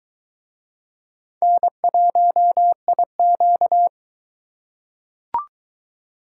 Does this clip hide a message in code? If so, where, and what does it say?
Yes: Morse code from 1.4 to 3.9 s, heard as a keyed tone, about 22 wpm, 700 Hz. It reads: N1IQ